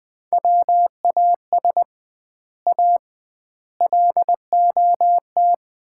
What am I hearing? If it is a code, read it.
Morse code, heard as a keyed tone, about 20 wpm, 700 Hz. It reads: WAS A LOT
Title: 2x Word-Spacing - Sets of 4 Words - 20wpm